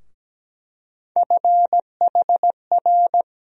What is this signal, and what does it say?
Morse code, heard as a keyed tone, about 17 wpm, 700 Hz. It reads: FHR
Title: USA Call Sign Postfixes - 3 Letters 17wpm